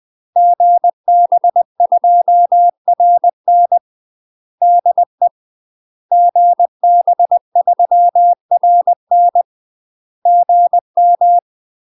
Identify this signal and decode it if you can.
Morse code, heard as a keyed tone, about 20 wpm, 700 Hz. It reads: GB2RN DE GB3RN GM
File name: UK MASTER QSOs TRAINING / 20WPM / UK MASTER QSOs 0027